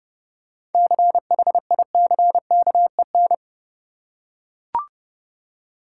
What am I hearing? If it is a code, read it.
Morse code, heard as a keyed tone, about 30 wpm, 700 Hz. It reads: CHICKEN
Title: Seven Letter Words - 30wpm